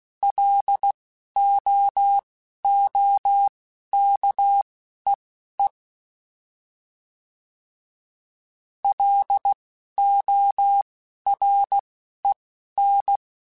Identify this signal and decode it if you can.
Morse code, heard as a keyed tone, about 16 wpm, spaced as for 11 wpm, 800 Hz. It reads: LOOKEE LOREN
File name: L Runs / L Runs mp3s / L Runs @10